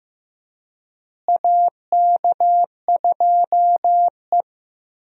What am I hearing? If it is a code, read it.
Morse code, heard as a keyed tone, about 15 wpm, 700 Hz. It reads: AK2E
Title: USA Call Signs - 4 Characters - Repeated 2x - 15wpm